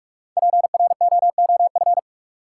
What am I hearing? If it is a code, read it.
Morse code, heard as a keyed tone, about 44 wpm, 700 Hz. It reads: PROOF
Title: Five Letter Words - 45wpm